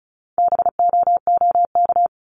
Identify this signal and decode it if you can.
Morse code, heard as a keyed tone, about 36 wpm, 700 Hz. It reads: BOOK
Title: Multi-letter Phonemes Repeated 3x 35wpm